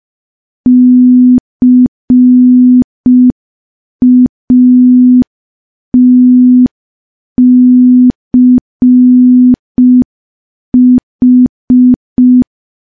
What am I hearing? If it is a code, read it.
Morse code, heard as a keyed tone, about 5 wpm, 250 Hz. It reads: CATCH
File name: morse / morse-training-05WPM